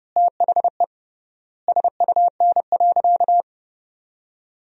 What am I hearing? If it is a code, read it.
Morse code, heard as a keyed tone, about 30 wpm, 700 Hz. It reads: THE SUN.
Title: 3x Word-Spacing - Sentences from Top 300 Words - 30wpm